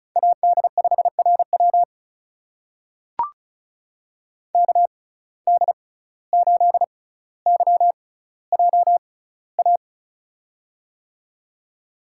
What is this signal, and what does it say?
Morse code, heard as a keyed tone, about 36 wpm, spaced as for 17 wpm, 700 Hz. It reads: AD5RW KD8YJA
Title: USA Call Signs - 6x Character Spacing - 35wpm